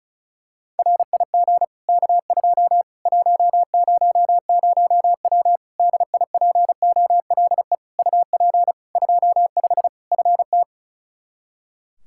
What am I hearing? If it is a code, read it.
Morse code, heard as a keyed tone, about 36 wpm, 700 Hz. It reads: RIG K2 100W DIPOLE UP 25 FT
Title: Practice QSO - Exchange 2 - Info - 35wpm